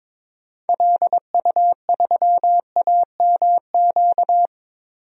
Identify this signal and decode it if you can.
Morse code, heard as a keyed tone, about 22 wpm, 700 Hz. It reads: LU3AMQ
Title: DXCC-100 Call Signs - 6 Characters - 22wpm